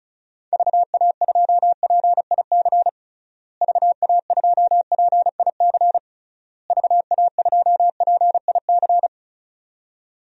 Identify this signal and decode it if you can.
Morse code, heard as a keyed tone, about 36 wpm, 700 Hz. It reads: VA2PIC VA2PIC VA2PIC
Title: DXCC-100 Call Signs - 6 Characters - Repeated 3x - 35wpm